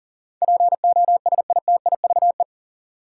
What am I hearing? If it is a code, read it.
Morse code, heard as a keyed tone, about 40 wpm, 700 Hz. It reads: POSITIVE
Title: Sentences from Top 1000 Words - Review Part 5 - New Word Drill - 40wpm